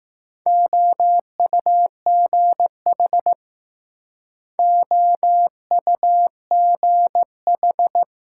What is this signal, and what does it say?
Morse code, heard as a keyed tone, about 21 wpm, 700 Hz. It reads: OUGH OUGH